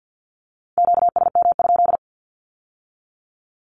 Morse code, heard as a keyed tone, about 50 wpm, 700 Hz. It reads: QSM?
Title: Q-Codes with Spoken Meaning - 50wpm